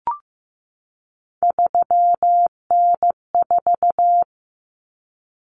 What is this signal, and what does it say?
Morse code, heard as a keyed tone, about 15 wpm, 700 Hz. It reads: 3N4
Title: Common Cut Numbers - 3-Digit 15wpm